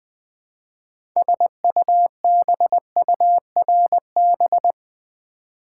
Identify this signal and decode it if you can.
Morse code, heard as a keyed tone, about 20 wpm, 700 Hz. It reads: SUBURB
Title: Build A Word - 2 Letters At A Time - 8-Letter Words - 20wpm